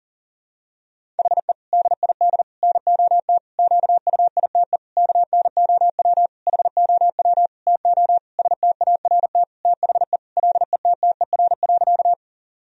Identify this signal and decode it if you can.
Morse code, heard as a keyed tone, about 40 wpm, 700 Hz. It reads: HE DID NOT QUITE KNOW HOW TO START THE LETTER.
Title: Sentences from Top 1000 Words - Review Part 1 - 40wpm